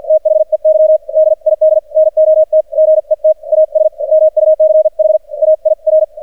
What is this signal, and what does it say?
Morse code, heard as a keyed tone, about 48 wpm, spaced as for 41 wpm, 620 Hz. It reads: THEY DID NOT GET AS MUCH AIR
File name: archived / CRSnet